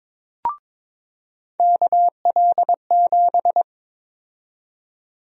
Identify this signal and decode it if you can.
Morse code, heard as a keyed tone, about 22 wpm, 700 Hz. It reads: KL7
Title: USA Call Sign Prefixes - 22wpm